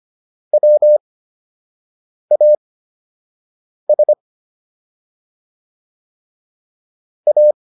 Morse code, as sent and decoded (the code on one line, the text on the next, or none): .-- .- ... / .-
WAS A